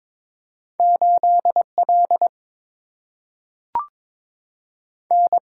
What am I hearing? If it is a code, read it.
Morse code, heard as a keyed tone, about 22 wpm, 700 Hz. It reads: 8 L N
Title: Single-Letter Single-Number 22wpm